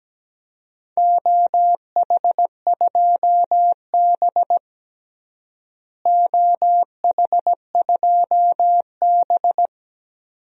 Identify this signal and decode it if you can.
Morse code, heard as a keyed tone, about 17 wpm, 700 Hz. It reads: OH2B OH2B
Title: DXCC-100 Call Signs - 4 Characters - Repeated 2x - 17wpm